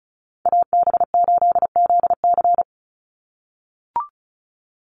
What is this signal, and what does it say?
Morse code, heard as a keyed tone, about 34 wpm, 700 Hz. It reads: AB8ZC
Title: USA Call Signs 35wpm